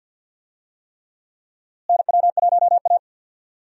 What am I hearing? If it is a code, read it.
Morse code, heard as a keyed tone, about 51 wpm, 700 Hz. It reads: NW1A
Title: USA Call Signs - 4 Characters - Repeated 3x - 50wpm